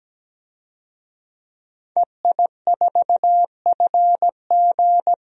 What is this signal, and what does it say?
Morse code, heard as a keyed tone, about 17 wpm, 700 Hz. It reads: EI4FG
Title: DXCC-100 Call Signs - 5 Characters - 17wpm